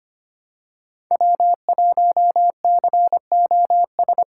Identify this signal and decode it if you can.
Morse code, heard as a keyed tone, about 25 wpm, 700 Hz. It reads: W1COS
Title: USA Call Signs - 5 Characters - 8x Character Spacing - 25wpm